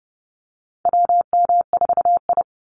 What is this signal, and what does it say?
Morse code, heard as a keyed tone, about 30 wpm, 700 Hz. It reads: WM4I
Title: Call Signs Repeated 3x - 30wpm